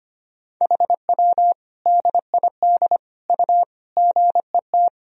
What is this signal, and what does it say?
Morse code, heard as a keyed tone, about 25 wpm, 700 Hz. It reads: HW DID U GET
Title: Sentences from Top 200 Words and Abbreviations 25wpm